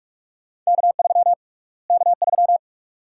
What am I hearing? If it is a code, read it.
Morse code, heard as a keyed tone, about 42 wpm, 700 Hz. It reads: K3 K3
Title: USA Call Sign Prefixes - Repeated 2x - 45wpm